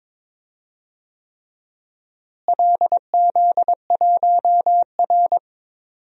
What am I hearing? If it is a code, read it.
Morse code, heard as a keyed tone, about 22 wpm, 700 Hz. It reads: LZ1R
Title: DXCC-100 Call Signs - 4 Characters - Repeated 3x - 22wpm